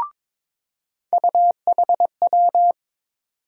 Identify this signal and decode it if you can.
Morse code, heard as a keyed tone, about 22 wpm, 700 Hz. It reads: UHW
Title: USA Call Sign Postfixes - 3 Letters 22wpm